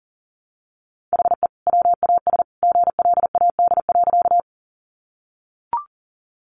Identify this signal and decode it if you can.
Morse code, heard as a keyed tone, about 40 wpm, 700 Hz. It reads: HE WAS GLAD.
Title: Sentences from Top 800 Words 40wpm